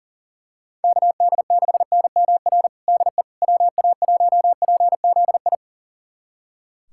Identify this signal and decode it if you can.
Morse code, heard as a keyed tone, about 40 wpm, 700 Hz. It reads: KD6NMR DE WA1PZI
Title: USA Callsign DE Callsign 40wpm